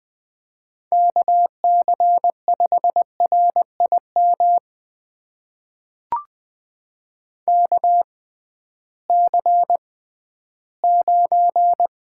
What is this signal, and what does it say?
Morse code, heard as a keyed tone, about 20 wpm, spaced as for 13 wpm, 700 Hz. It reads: KC5RIM KC9
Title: USA Call Signs - 6 Characters - 6x Character Spacing - 20wpm